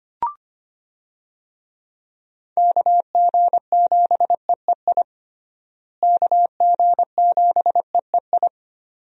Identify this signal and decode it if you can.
Morse code, heard as a keyed tone, about 25 wpm, 700 Hz. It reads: KG7EEI KG7EEI
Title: USA Call Signs - 6 Characters - Repeated 2x - 25wpm